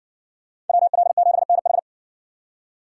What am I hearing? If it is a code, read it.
Morse code, heard as a keyed tone, about 60 wpm, 700 Hz. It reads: WL7NH